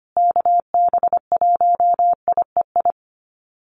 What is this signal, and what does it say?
Morse code, heard as a keyed tone, about 25 wpm, 700 Hz. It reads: KB1IEI
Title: USA Call Signs Repeated 2x 25wpm